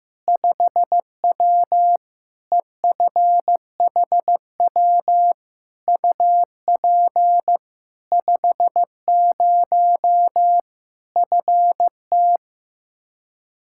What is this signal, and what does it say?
Morse code, heard as a keyed tone, about 15 wpm, 700 Hz. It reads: 5W EFHW UP 50 FT
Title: Practice QSO - Exchange 2 - Info - 15wpm